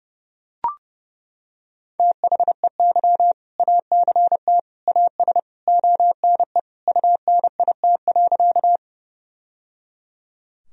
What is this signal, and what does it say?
Morse code, heard as a keyed tone, about 30 wpm, 700 Hz. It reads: THEY ACT AS ONE UNIT.
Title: Sentences from Top 1000 Words - Review Part 2 - 30wpm